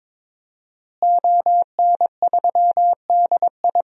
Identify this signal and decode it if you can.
Morse code, heard as a keyed tone, about 23 wpm, 700 Hz. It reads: ON3DI
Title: DXCC-100 Call Signs - 5 Characters - Repeated 3x - 22wpm